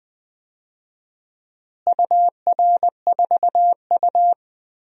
Morse code, heard as a keyed tone, about 21 wpm, 700 Hz. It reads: UR4U
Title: DXCC-100 Call Signs - 4 Characters - Repeated 2x - 20wpm